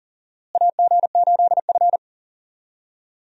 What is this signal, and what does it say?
Morse code, heard as a keyed tone, about 40 wpm, 700 Hz. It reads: AG8F
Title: USA Call Signs - 4 Characters - Repeated 3x - 40wpm